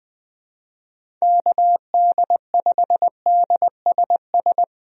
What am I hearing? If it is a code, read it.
Morse code, heard as a keyed tone, about 21 wpm, 700 Hz. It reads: KD5DSS
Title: USA Call Signs - 6 Characters - Repeated 2x - 20wpm